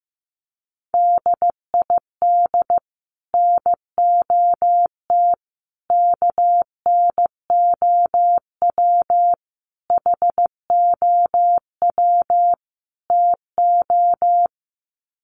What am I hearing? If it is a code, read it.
Morse code, heard as a keyed tone, about 15 wpm, 700 Hz. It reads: DID NOT KNOW HOW TO
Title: Sets of 5 Words 15wpm